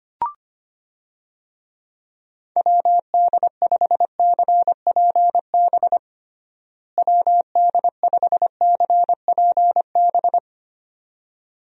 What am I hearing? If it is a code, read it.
Morse code, heard as a keyed tone, about 25 wpm, 700 Hz. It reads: WD5CPB WD5CPB
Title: USA Call Signs - 6 Characters - Repeated 2x - 25wpm